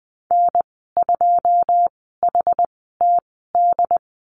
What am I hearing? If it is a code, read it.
Morse code, heard as a keyed tone, about 20 wpm, spaced as for 14 wpm, 700 Hz. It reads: N2HTD
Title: 147 Lesson 24g - Letter y - Call Signs 20wpm